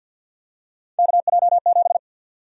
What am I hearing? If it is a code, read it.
Morse code, heard as a keyed tone, about 51 wpm, 700 Hz. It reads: KJ7
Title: USA Call Sign Prefixes - 50wpm